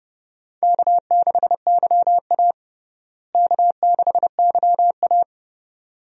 Morse code, heard as a keyed tone, about 30 wpm, 700 Hz. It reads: K6YA K6YA
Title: USA Call Signs - 4 Characters - Repeated 2x - 30wpm